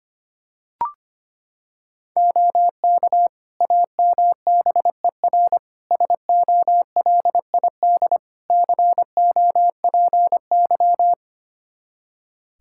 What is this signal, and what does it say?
Morse code, heard as a keyed tone, about 25 wpm, 700 Hz. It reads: OK AMBER SOLID COPY
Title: Practice QSO - Exchange 3 - More Info - 25wpm